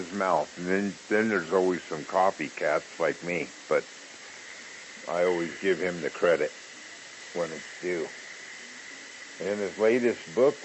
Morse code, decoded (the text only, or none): none